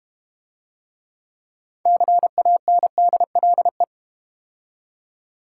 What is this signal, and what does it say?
Morse code, heard as a keyed tone, about 32 wpm, 700 Hz. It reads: CANDLE